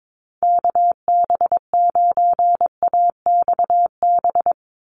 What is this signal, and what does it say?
Morse code, heard as a keyed tone, about 22 wpm, 700 Hz. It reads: KB9AXB
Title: USA Call Signs 22wpm